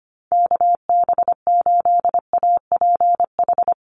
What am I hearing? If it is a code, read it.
Morse code, heard as a keyed tone, about 25 wpm, 700 Hz. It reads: KB8APH
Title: USA Call Signs Repeated 3x 25wpm